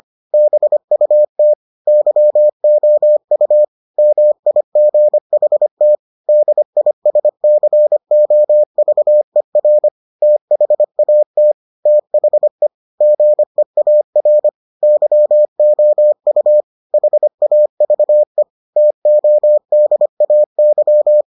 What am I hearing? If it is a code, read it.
Morse code, heard as a keyed tone, about 25 wpm, 600 Hz. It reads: BUT YOU MIGHT DISCOVER THAT THE GEAR YOU HAVE TODAY